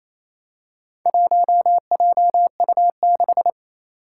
Morse code, heard as a keyed tone, about 28 wpm, 700 Hz. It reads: 1JU6
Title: Four Challenging Letter-Number Combinations 28wpm